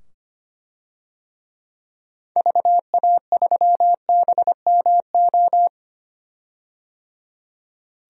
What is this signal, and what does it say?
Morse code, heard as a keyed tone, about 25 wpm, 700 Hz. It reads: VA3BMO
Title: DXCC-100 Call Signs - 6 Characters - 25wpm